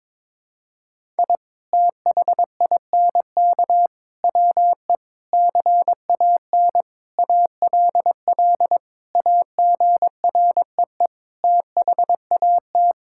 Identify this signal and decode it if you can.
Morse code, heard as a keyed tone, about 22 wpm, 700 Hz. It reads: I THINK WE CAN ALL AGREE THAT
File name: practice / 22wpm / Sets of 7 Words - 22wpm